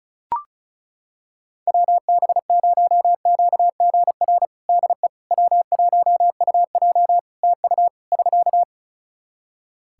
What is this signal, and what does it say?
Morse code, heard as a keyed tone, about 36 wpm, 700 Hz. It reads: WB0QGR DE W1UJ TU <SK>
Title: Practice QSO - Exchange 4 - Ending - 35wpm